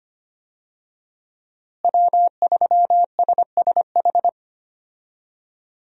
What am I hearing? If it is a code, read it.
Morse code, heard as a keyed tone, about 25 wpm, 700 Hz. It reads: W3SSH
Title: USA Call Signs - 5 Characters - Repeated 2x - 25wpm